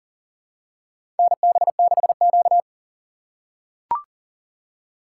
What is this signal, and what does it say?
Morse code, heard as a keyed tone, about 40 wpm, 700 Hz. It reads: NB6Q
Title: USA Call Signs - 4 Characters - 3x Character Spacing - 40wpm